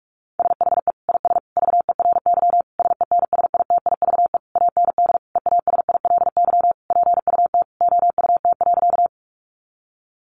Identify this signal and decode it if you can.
Morse code, heard as a keyed tone, about 44 wpm, 700 Hz. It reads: SHE IS VERY SENSITIVE AND EASILY PUT OUT.
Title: Sentences from Top 1400 Words 45wpm